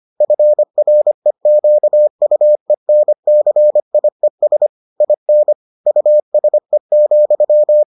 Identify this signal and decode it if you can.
Morse code, heard as a keyed tone, about 25 wpm, 600 Hz. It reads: FREQUENCIES IN USE,